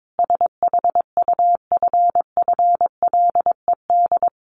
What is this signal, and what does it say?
Morse code, heard as a keyed tone, about 22 wpm, 700 Hz. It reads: SHUFFLED
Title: A Sound Of Thunder 22wpm